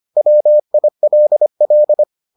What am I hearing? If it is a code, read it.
Morse code, heard as a keyed tone, about 25 wpm, 600 Hz. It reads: WILL